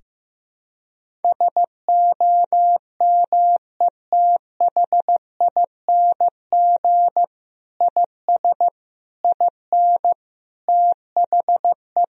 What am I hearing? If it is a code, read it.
Morse code, heard as a keyed tone, about 16 wpm, 700 Hz. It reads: SOMETHING IS IN THE
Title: Sentences from Top 1000 Words - Review Part 1 - 15wpm